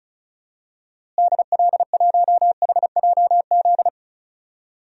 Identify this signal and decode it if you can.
Morse code, heard as a keyed tone, about 35 wpm, 700 Hz. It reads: DL1HJZ
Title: DXCC-100 Call Signs - 6 Characters - Repeated 3x - 35wpm